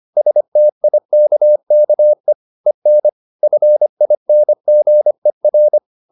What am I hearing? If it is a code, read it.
Morse code, heard as a keyed tone, about 25 wpm, 600 Hz. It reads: STIKKE EN FINGER
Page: Ferie 25 wpm